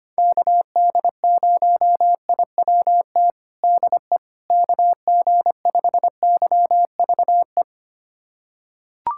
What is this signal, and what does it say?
Morse code, heard as a keyed tone, about 25 wpm, 700 Hz. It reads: KD0IWT DE KG5YVE
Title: USA Callsign DE Callsign - 6 Characters - 25wpm